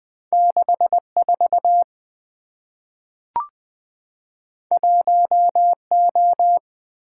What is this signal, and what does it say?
Morse code, heard as a keyed tone, about 20 wpm, 700 Hz. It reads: 64 1O
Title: Two Letter-Number Combinations - 20wpm